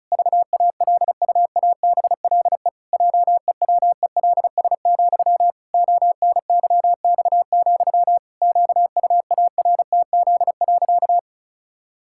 Morse code, heard as a keyed tone, about 36 wpm, 700 Hz. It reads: VALUABLE JEWELS, ONYX, QUARTZ.
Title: Pangram Challenge v2 - 35wpm